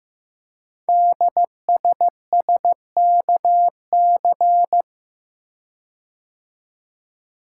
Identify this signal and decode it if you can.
Morse code, heard as a keyed tone, about 15 wpm, 700 Hz. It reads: DSSKC
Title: Code Groups - 5 Letters - 15wpm